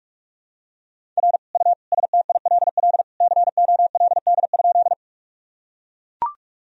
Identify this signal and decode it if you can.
Morse code, heard as a keyed tone, about 42 wpm, 700 Hz. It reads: R U STILL COLD?
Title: Sentences from Top 400 Words and Abbreviations 45wpm